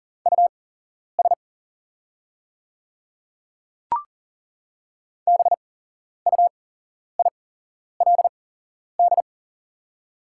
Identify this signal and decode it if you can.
Morse code, heard as a keyed tone, about 40 wpm, spaced as for 11 wpm, 700 Hz. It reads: US BUILD
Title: 8x Character Spacing - Top 200 Words - 40wpm